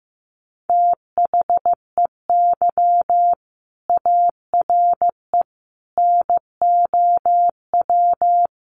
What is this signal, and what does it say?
Morse code, heard as a keyed tone, about 15 wpm, 700 Hz. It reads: THEY ARE NOW
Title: Sentences from Top 900 Words 15wpm